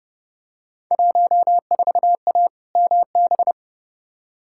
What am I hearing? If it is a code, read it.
Morse code, heard as a keyed tone, about 30 wpm, 700 Hz. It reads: 14A MB
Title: Field Day Exchanges Encore - 30wpm